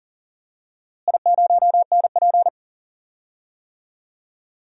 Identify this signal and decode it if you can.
Morse code, heard as a keyed tone, about 37 wpm, 700 Hz. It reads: I0NP
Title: DXCC-100 Call Signs - 4 Characters - Repeated 3x - 40wpm